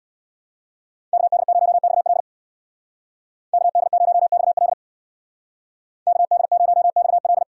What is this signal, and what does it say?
Morse code, heard as a keyed tone, about 71 wpm, 700 Hz. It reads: KD0XL KD0XL KD0XL